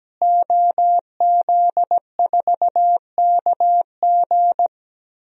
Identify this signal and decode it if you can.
Morse code, heard as a keyed tone, about 17 wpm, 700 Hz. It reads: OZ4KG
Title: DXCC-100 Call Signs - 5 Characters - Repeated 3x - 17wpm